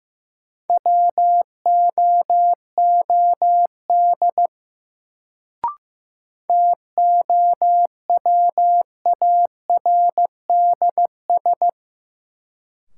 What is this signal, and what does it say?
Morse code, heard as a keyed tone, about 15 wpm, 700 Hz. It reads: WOOD TOWARDS
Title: Sentences from Top 1000 Words - Review Part 2 - New Word Drill - 15wpm